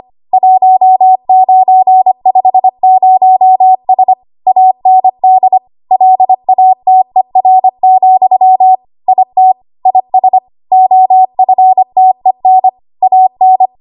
Morse code, heard as a keyed tone, about 25 wpm, 750 Hz. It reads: 1950S AND LATER, IT IS OFTEN AN
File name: Archive / 25 WPM / 25WPM